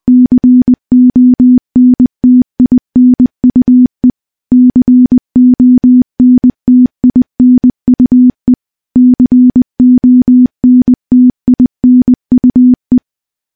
Morse code, heard as a keyed tone, about 19 wpm, 250 Hz. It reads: CONTINUE CONTINUE CONTINUE
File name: morse-training-20WPM